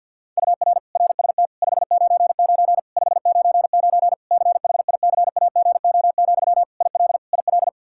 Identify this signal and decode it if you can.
Morse code, heard as a keyed tone, about 47 wpm, 700 Hz. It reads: UR RST 599 599 CHICAGO, IL IL
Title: Practice QSO - Exchange 1 - Beginning - 50wpm